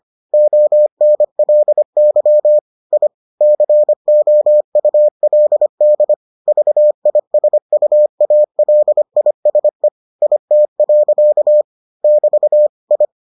Morse code, heard as a keyed tone, about 25 wpm, 600 Hz. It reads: ONLY I COULD VISUALISE IT. <BT> I